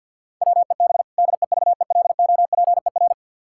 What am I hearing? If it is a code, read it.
Morse code, heard as a keyed tone, about 50 wpm, 700 Hz. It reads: WEB DEVELOPER
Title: Practice QSO - Exchange 3 - More Info - 50wpm